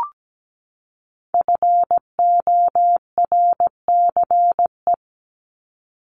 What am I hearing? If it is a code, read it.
Morse code, heard as a keyed tone, about 17 wpm, 700 Hz. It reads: FORCE
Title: ED Suffix - Top 100 Words with Base Words 17wpm